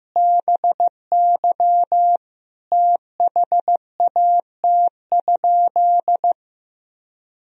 Morse code, heard as a keyed tone, about 15 wpm, 700 Hz. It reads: BY THAT?
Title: Sentences from Top 200 Words and Abbreviations 15wpm